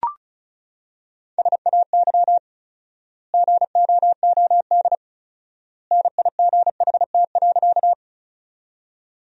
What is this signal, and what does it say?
Morse code, heard as a keyed tone, about 36 wpm, 700 Hz. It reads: SAY GOOD NIGHT.
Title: 4x Word-Spacing - Sentences from Top 300 Words - 35wpm